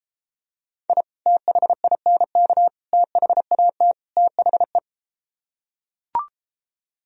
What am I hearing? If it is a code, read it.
Morse code, heard as a keyed tone, about 33 wpm, 700 Hz. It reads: I THINK THAT THE